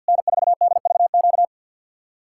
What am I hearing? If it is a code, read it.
Morse code, heard as a keyed tone, about 61 wpm, 700 Hz. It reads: N4DVQ
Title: Speed-Racing Common QSO Phrases Encore - 50wpm